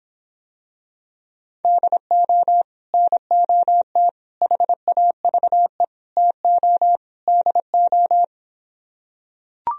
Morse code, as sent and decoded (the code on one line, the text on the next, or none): -.. --- / -. --- - / .... .- ...- . / - --- / -.. ---
DO NOT HAVE TO DO